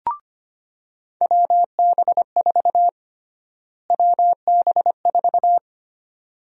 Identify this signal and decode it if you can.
Morse code, heard as a keyed tone, about 25 wpm, 700 Hz. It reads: WB4 WB4
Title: USA Call Sign Prefixes - Repeated 2x - 25wpm